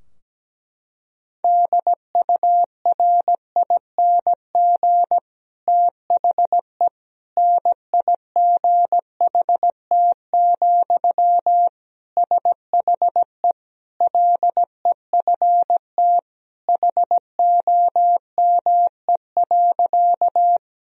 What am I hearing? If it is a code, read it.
Morse code, heard as a keyed tone, about 17 wpm, 700 Hz. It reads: DURING THE NIGHT, SHE LEFT HOME.
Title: Sentences from Top 1000 Words - Review Part 2 - 17wpm